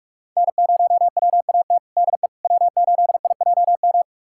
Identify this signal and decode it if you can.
Morse code, heard as a keyed tone, about 41 wpm, 700 Hz. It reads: N0WAT DE W8IJM
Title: USA Callsign DE Callsign - 5 Characters - 45wpm